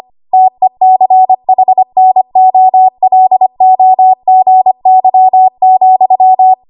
Morse code, heard as a keyed tone, about 25 wpm, 750 Hz. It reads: TECHNOLOGY,
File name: Archive / 25 WPM / 25WPM